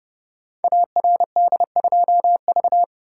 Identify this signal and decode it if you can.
Morse code, heard as a keyed tone, about 30 wpm, 700 Hz. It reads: ARD2V
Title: Code Groups - 5 Letter:Numbers - 30wpm